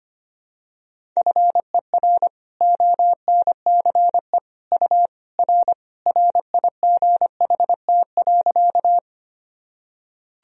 Morse code, heard as a keyed tone, about 25 wpm, 700 Hz. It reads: FER ONCE U R RIGHT.
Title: Sentences from Top 300 Words and Abbreviations 25wpm